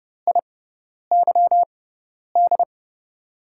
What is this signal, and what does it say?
Morse code, heard as a keyed tone, about 30 wpm, spaced as for 10 wpm, 700 Hz. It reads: IYD
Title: USA Call Signs - 6 Characters - 6x Character Spacing - 30wpm